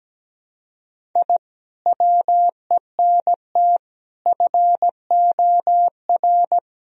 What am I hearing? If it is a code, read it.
Morse code, heard as a keyed tone, about 17 wpm, 700 Hz. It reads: I WENT FOR
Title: Sentences from Top 1000 Words - Review Part 1 - 17wpm